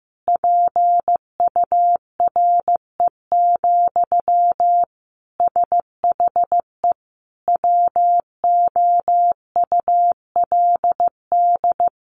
Morse code, heard as a keyed tone, about 15 wpm, 700 Hz. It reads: PURE, SHE WOULD
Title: Sentences from Top 1500 Words 15wpm